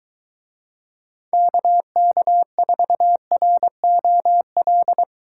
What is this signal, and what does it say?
Morse code, heard as a keyed tone, about 23 wpm, 700 Hz. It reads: KK4ROL